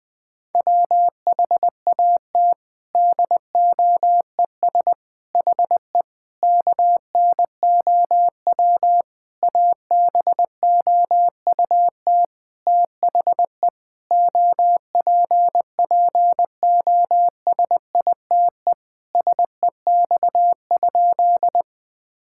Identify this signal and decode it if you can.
Morse code, heard as a keyed tone, about 21 wpm, 700 Hz. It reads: WHAT DOES HE KNOW ABOUT THE OPPOSITE SEX?
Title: Sentences from Top 1600 Words 20wpm